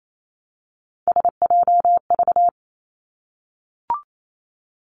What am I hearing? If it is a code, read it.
Morse code, heard as a keyed tone, about 28 wpm, 700 Hz. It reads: SJV
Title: ARRL Sections 28wpm